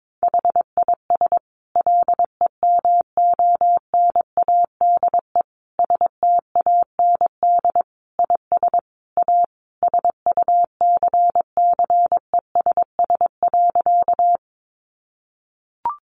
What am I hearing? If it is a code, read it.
Morse code, heard as a keyed tone, about 22 wpm, 700 Hz. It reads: HIS LEMONADE STAND IS A SUCCESS.
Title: Additional Easy Phrases 22wpm